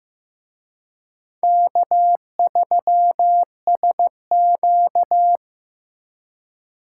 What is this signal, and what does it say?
Morse code, heard as a keyed tone, about 18 wpm, 700 Hz. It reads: K3SQ
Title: Speed Racing - USA 1x2 Call Signs 15wpm